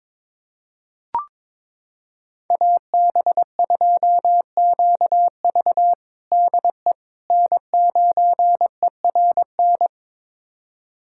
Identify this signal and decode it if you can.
Morse code, heard as a keyed tone, about 22 wpm, 700 Hz. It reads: AB2QV DE N9ERN
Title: USA Callsign DE Callsign - 5 Characters - 22wpm